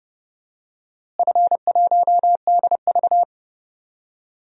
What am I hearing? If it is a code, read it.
Morse code, heard as a keyed tone, about 32 wpm, 700 Hz. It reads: F1DV
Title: DXCC-100 Call Signs - 4 Characters - Repeated 2x - 30wpm